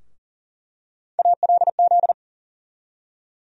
Morse code, heard as a keyed tone, about 40 wpm, 700 Hz. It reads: ALZ
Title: USA Call Sign Postfixes - 3 Letters 40wpm